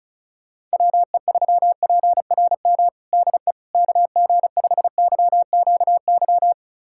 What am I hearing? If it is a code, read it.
Morse code, heard as a keyed tone, about 36 wpm, 700 Hz. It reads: WE3PRM DE KG5YQY
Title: USA Callsign DE Callsign - 6 Characters - 35wpm